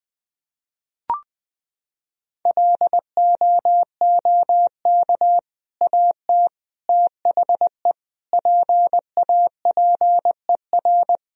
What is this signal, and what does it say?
Morse code, heard as a keyed tone, about 20 wpm, 700 Hz. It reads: LOOK AT THE PAPER
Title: Sentences from Top 400 Words and Abbreviations 20wpm